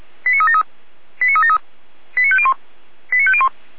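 Five-tone selective call (ZVEI), heard as quick runs of five tones, tones of about 70 ms. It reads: 89373 89373 89501 89501
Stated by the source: ZVEI five-tone selective call received on an SDR.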